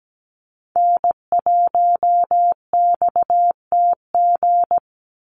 Morse code, heard as a keyed tone, about 17 wpm, 700 Hz. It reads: N1XTG
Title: USA Call Signs 17wpm